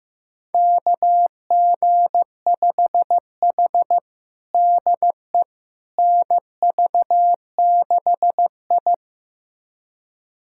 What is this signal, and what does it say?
Morse code, heard as a keyed tone, about 15 wpm, 700 Hz. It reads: KG5H DE NV6I
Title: USA Callsign DE Callsign - 4 Characters - 15wpm